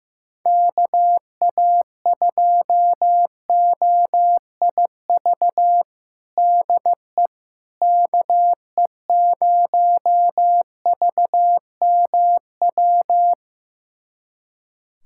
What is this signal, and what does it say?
Morse code, heard as a keyed tone, about 16 wpm, 700 Hz. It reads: KA2OIV DE KE0VMW
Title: USA Callsign DE Callsign - 6 Characters - 15wpm